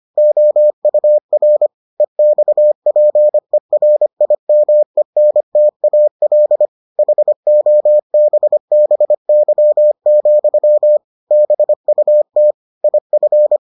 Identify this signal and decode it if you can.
Morse code, heard as a keyed tone, about 25 wpm, 600 Hz. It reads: OUR EXPERIMENTAL HOBBY, BUT IF